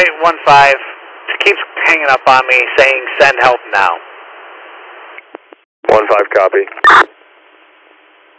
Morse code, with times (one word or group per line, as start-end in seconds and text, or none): none